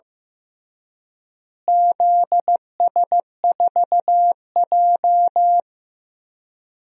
Morse code, heard as a keyed tone, about 15 wpm, 700 Hz. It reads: ZS4J
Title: DXCC-100 Call Signs - 4 Characters - Repeated 3x - 15wpm